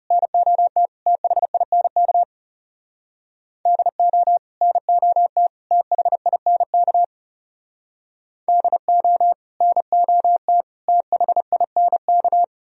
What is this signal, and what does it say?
Morse code, heard as a keyed tone, about 38 wpm, 700 Hz. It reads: NOT THINK DO NOT THINK DO NOT THINK